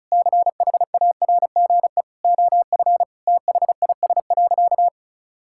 Morse code, heard as a keyed tone, about 36 wpm, 700 Hz. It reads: CHARGE OF THIS.
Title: Sentences from Top 1000 Words - Review Part 6 - 35wpm